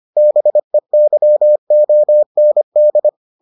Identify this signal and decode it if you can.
Morse code, heard as a keyed tone, about 25 wpm, 600 Hz. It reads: BEYOND